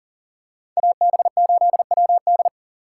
Morse code, heard as a keyed tone, about 40 wpm, 700 Hz. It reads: AB8WD
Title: USA Call Signs - 5 Characters - 3x Character Spacing - 40wpm